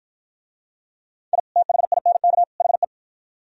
Morse code, heard as a keyed tone, about 52 wpm, 700 Hz. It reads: I THINK HE